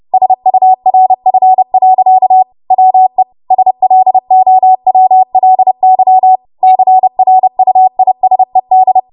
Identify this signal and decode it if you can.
Morse code, heard as a keyed tone, about 30 wpm, 750 Hz. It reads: SURF. WE SLOWLY CRUISED
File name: Morse / 30WPM